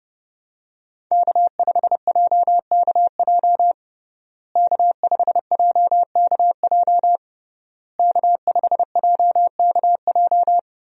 Morse code, heard as a keyed tone, about 32 wpm, 700 Hz. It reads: K5JKJ K5JKJ K5JKJ
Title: USA Call Signs - 5 Characters - Repeated 3x - 30wpm